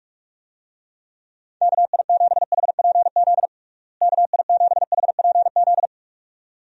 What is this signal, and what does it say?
Morse code, heard as a keyed tone, about 46 wpm, 700 Hz. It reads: KI7HPZ KI7HPZ
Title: USA Call Signs - 6 Characters - Repeated 2x - 45wpm